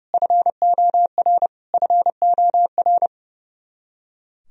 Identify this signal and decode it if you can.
Morse code, heard as a keyed tone, about 30 wpm, 700 Hz. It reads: FOR FOR
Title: Top 100 Words Repeated 30wpm